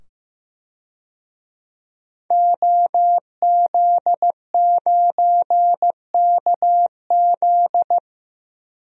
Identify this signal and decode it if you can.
Morse code, heard as a keyed tone, about 15 wpm, 700 Hz. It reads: OZ9KZ
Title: DXCC-100 Call Signs - 5 Characters - 15wpm